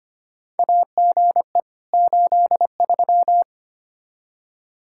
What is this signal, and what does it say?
Morse code, heard as a keyed tone, about 25 wpm, 700 Hz. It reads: AGE 83
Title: Practice QSO - Exchange 3 - More Info - 25wpm